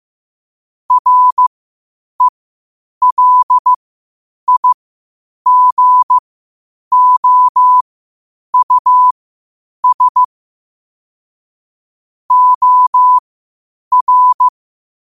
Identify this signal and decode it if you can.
Morse code, heard as a keyed tone, about 15 wpm, spaced as for 8 wpm, 1000 Hz. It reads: RELIGOUS OR
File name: qotdfiles / 8wpm